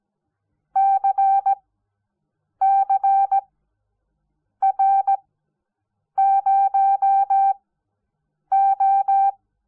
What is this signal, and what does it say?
Morse code, heard as a keyed tone, about 17 wpm, spaced as for 7 wpm, 780 Hz. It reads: CCR0O
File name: Onlinelehrgang / morsen / DJ4UF Morsen05